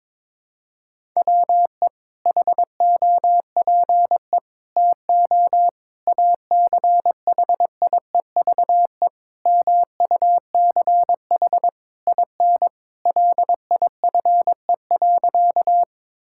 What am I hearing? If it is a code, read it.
Morse code, heard as a keyed tone, about 22 wpm, 700 Hz. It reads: WE HOPE TO ACHIEVE MUCH IN LIFE.
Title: Sentences from Top 1000 Words - Review Part 6 - 22wpm